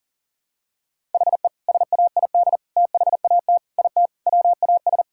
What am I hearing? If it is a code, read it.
Morse code, heard as a keyed tone, about 40 wpm, 700 Hz. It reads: HE SAID THAT IT WAS
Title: Sentences from Top 1000 Words - Review Part 6 - 40wpm